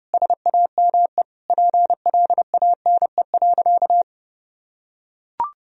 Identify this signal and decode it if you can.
Morse code, heard as a keyed tone, about 30 wpm, 700 Hz. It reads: SAME PLANE.
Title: Sentences from Top 1000 Words - Review Part 3 - 30wpm